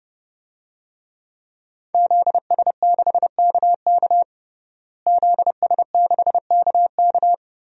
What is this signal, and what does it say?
Morse code, heard as a keyed tone, about 30 wpm, 700 Hz. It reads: ZS6KK ZS6KK
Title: DXCC-100 Call Signs - 5 Characters - Repeated 3x - 30wpm